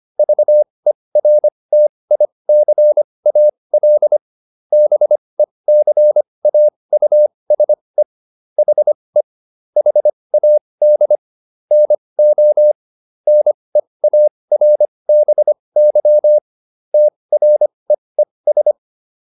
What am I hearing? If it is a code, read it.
Morse code, heard as a keyed tone, about 25 wpm, spaced as for 20 wpm, 600 Hz. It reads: VERTICAL BECAUSE HE HAD NO NEARBY TREES